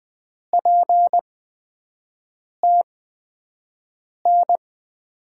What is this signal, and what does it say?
Morse code, heard as a keyed tone, about 20 wpm, spaced as for 5 wpm, 700 Hz. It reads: PTN
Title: USA Call Signs - 6 Characters - 8x Character Spacing - 20wpm